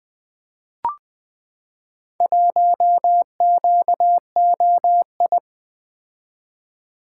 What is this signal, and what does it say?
Morse code, heard as a keyed tone, about 20 wpm, 700 Hz. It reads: 1QOI
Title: Four Letter-Number Combinations - 20wpm